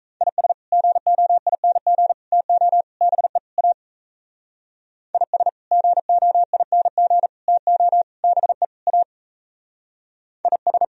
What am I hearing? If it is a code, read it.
Morse code, heard as a keyed tone, about 36 wpm, 700 Hz. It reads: IS GOING TO BE A IS GOING TO BE A IS